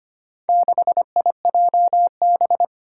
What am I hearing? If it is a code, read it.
Morse code, heard as a keyed tone, about 25 wpm, 700 Hz. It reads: 6IJB
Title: Four Challenging Letter-Number Combinations 25wpm